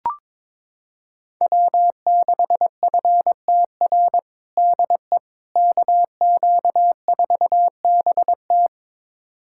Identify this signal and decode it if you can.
Morse code, heard as a keyed tone, about 22 wpm, 700 Hz. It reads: W6FTR DE KQ4BT
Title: USA Callsign DE Callsign - 5 Characters - 22wpm